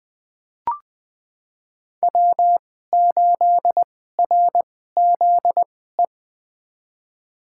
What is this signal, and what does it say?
Morse code, heard as a keyed tone, about 20 wpm, spaced as for 15 wpm, 700 Hz. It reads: W8RZE
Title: USA Call Signs - 2x Character Spacing - 20wpm